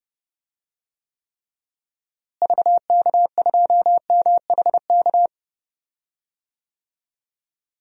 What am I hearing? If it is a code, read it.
Morse code, heard as a keyed tone, about 30 wpm, 700 Hz. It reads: VK2MHK
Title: DXCC-100 Call Signs - 6 Characters - Repeated 2x - 30wpm